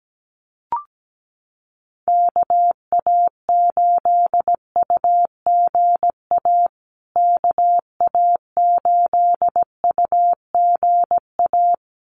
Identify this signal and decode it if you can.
Morse code, heard as a keyed tone, about 17 wpm, 700 Hz. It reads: KA8UGA KA8UGA
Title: USA Call Signs Repeated 2x 17wpm